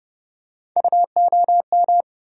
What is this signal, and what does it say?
Morse code, heard as a keyed tone, about 30 wpm, 700 Hz. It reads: UOM
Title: USA Call Sign Postfixes - 3 Letters 30wpm